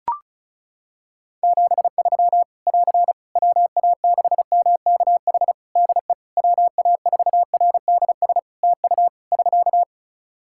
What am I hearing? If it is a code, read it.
Morse code, heard as a keyed tone, about 36 wpm, 700 Hz. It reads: 73 <AR> WA6MKH DE WA4RDS TU <SK>
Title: Practice QSO - Exchange 4 - Ending - 35wpm